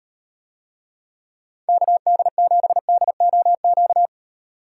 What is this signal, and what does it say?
Morse code, heard as a keyed tone, about 38 wpm, 700 Hz. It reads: KD7DOQ